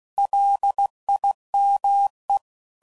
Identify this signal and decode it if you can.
Morse code, heard as a keyed tone, about 16 wpm, 800 Hz. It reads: LIME
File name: CW Code Groups @15WPM / fromSM's / Cities